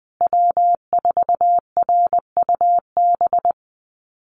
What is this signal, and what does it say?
Morse code, heard as a keyed tone, about 20 wpm, 700 Hz. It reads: W4RUB
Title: USA Call Signs Repeated 2x 20wpm